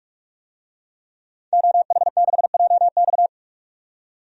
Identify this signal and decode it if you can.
Morse code, heard as a keyed tone, about 42 wpm, 700 Hz. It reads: OH6JX
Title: DXCC-100 Call Signs - 5 Characters - 45wpm